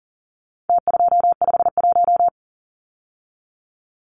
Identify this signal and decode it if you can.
Morse code, heard as a keyed tone, about 40 wpm, 700 Hz. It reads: T251
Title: Numbers - Contest Style - 4-Digit 40wpm